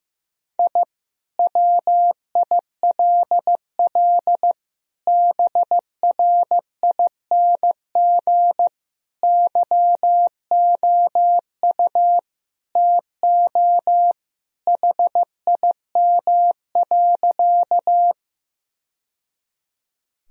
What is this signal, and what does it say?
Morse code, heard as a keyed tone, about 15 wpm, 700 Hz. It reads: I WILL BRING YOU TO HIM.
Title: Sentences from Top 1000 Words - Review Part 3 - 15wpm